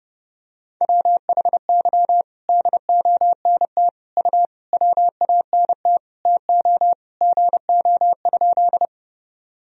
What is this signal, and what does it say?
Morse code, heard as a keyed tone, about 30 wpm, 700 Hz. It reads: WHY DONT U WANT TO GO?
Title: Sentences from Top 300 Words and Abbreviations 30wpm